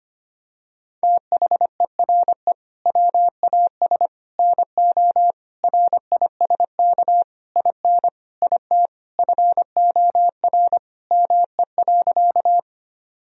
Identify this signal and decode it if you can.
Morse code, heard as a keyed tone, about 25 wpm, 700 Hz. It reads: THERE WAS NO RISK IN IT FOR ME.
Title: Sentences from Top 1000 Words - Review Part 4 - 25wpm